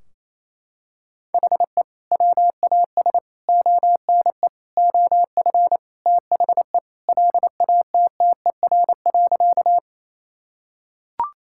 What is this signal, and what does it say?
Morse code, heard as a keyed tone, about 27 wpm, 700 Hz. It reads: HE WAS ONE OF THE LATTER.
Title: Sentences from Top 1600 Words 28wpm